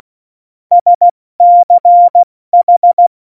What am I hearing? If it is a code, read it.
Morse code, heard as a keyed tone, about 16 wpm, 700 Hz. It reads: SCH
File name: Morsekurs / Witze / Witz 03. November 2020, 70 F80